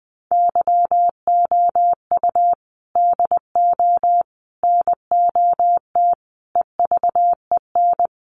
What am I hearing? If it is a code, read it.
Morse code, heard as a keyed tone, about 20 wpm, 700 Hz. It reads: YOU DO NOT EVEN
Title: Sets of 5 Words Encore2 20wpm